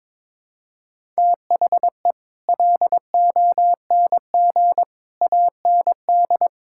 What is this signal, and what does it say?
Morse code, heard as a keyed tone, about 22 wpm, 700 Hz. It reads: THE LONG AND
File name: practice / 22wpm / Sentences with 8-Word Phrases - 22wpm